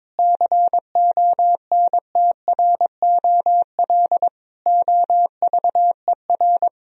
Morse code, heard as a keyed tone, about 22 wpm, 700 Hz. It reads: CONTROL OVER
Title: Sentences from Top 1000 Words - Review Part 4 - 22wpm